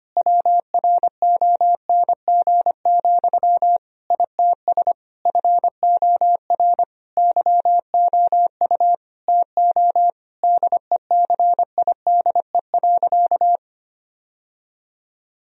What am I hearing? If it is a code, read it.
Morse code, heard as a keyed tone, about 25 wpm, 700 Hz. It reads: WRONG, ITS FOR YOU TO DECIDE.
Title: Sentences from Top 1000 Words - Review Part 3 - 25wpm